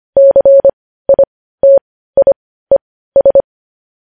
Morse code, heard as a keyed tone, about 25 wpm, spaced as for 15 wpm, 550 Hz. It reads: CITIES
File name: podcast / News headlines-1x-25x15